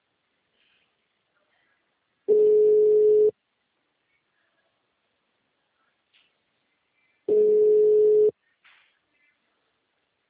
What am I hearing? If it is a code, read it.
Ringback tone (Europe, 425 Hz).